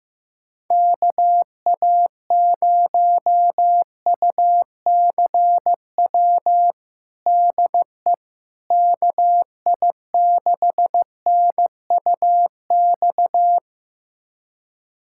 Morse code, heard as a keyed tone, about 15 wpm, 700 Hz. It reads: KA0UCW DE KI6NUX
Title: USA Callsign DE Callsign - 6 Characters - 15wpm